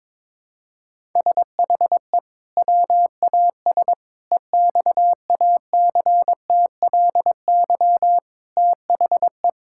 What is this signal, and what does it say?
Morse code, heard as a keyed tone, about 22 wpm, 700 Hz. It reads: SHE WAS EXACTLY THE
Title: Sentences from Top 1000 Words - Review Part 6 - 22wpm